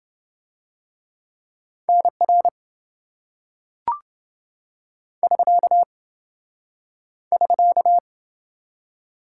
Morse code, heard as a keyed tone, about 25 wpm, 700 Hz. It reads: NR <SK> <SK>
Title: Speed-Racing Top 90 QSO Elements 20wpm